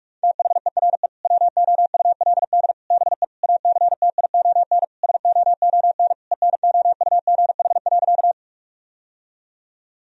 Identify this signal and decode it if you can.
Morse code, heard as a keyed tone, about 41 wpm, 700 Hz. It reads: THERE WOULD BE ACTION SOON ENOUGH.
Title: Sentences from Top 1000 Words - Review Part 4 - 45wpm